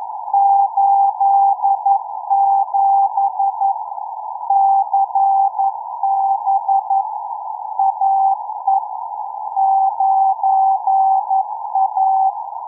Morse code, heard as a keyed tone, about 11 wpm, 780 Hz. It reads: 87 CB AE 9A